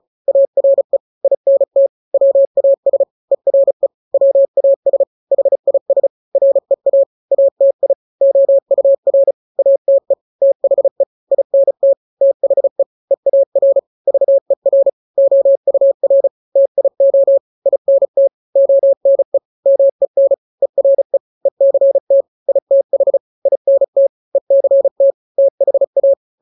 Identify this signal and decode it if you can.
Morse code, heard as a keyed tone, about 34 wpm, 550 Hz. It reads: ARE INT WAS ERE WAS HIS REA ATI OUR ATE THE INT THE EAR VER OUR TIO INT ONE MEN ERE ECT ITH INT ECT THA
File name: 3-grams 35 30 WPM